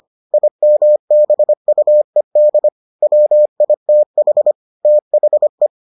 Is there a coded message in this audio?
Yes: Morse code from 0.3 to 5.7 s, heard as a keyed tone, about 25 wpm, 600 Hz. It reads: IMBUED WITH THE